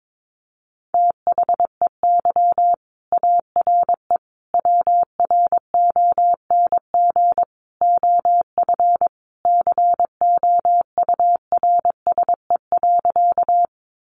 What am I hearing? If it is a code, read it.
Morse code, heard as a keyed tone, about 22 wpm, 700 Hz. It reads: THEY ARE WRONG OF COURSE.
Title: Sentences from Top 900 Words 22wpm